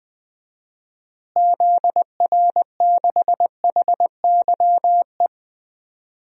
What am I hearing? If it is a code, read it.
Morse code, heard as a keyed tone, about 20 wpm, 700 Hz. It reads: ZR6HYE
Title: DXCC-100 Call Signs - 6 Characters - Repeated 2x - 20wpm